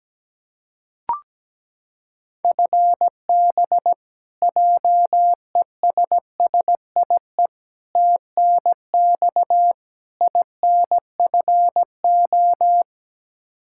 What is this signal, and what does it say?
Morse code, heard as a keyed tone, about 17 wpm, 700 Hz. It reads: FB JESSIE TNX INFO
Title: Practice QSO - Exchange 3 - More Info - 17wpm